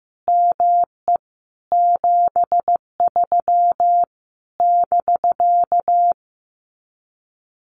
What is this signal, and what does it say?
Morse code, heard as a keyed tone, about 15 wpm, 700 Hz. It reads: ME 73 <BK>
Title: Common QSO Phrases Encore 15wpm